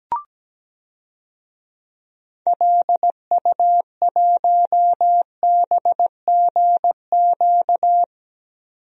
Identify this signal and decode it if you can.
Morse code, heard as a keyed tone, about 17 wpm, 700 Hz. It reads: LU1BGQ
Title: DXCC-100 Call Signs - 6 Characters - 17wpm